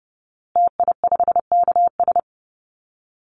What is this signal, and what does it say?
Morse code, heard as a keyed tone, about 30 wpm, 700 Hz. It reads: TI5KS
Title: Call Signs Repeated 2x - 30wpm